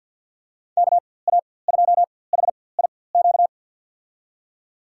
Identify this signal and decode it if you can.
Morse code, heard as a keyed tone, about 52 wpm, spaced as for 23 wpm, 700 Hz. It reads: KA2HIQ
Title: USA Call Signs - 6 Characters - 4x Character Spacing - 50wpm